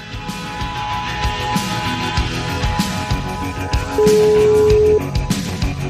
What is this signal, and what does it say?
Ringback tone (Europe, 425 Hz), with noise.